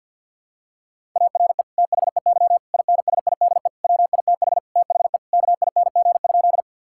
Morse code, heard as a keyed tone, about 52 wpm, 700 Hz. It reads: ARE THEY INSIDE WITH THE KING?
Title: Sentences from Top 1000 Words - Review Part 6 - 50wpm